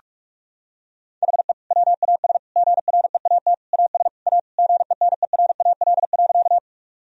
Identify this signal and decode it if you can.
Morse code, heard as a keyed tone, about 46 wpm, 700 Hz. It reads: HE WAS GREAT AS A GENERAL.
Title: Sentences from Top 1000 Words - Review Part 3 - 45wpm